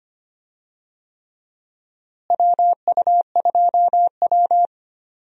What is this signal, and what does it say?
Morse code, heard as a keyed tone, about 25 wpm, 700 Hz. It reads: WU2W
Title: USA Call Signs - 4 Characters - Repeated 3x - 25wpm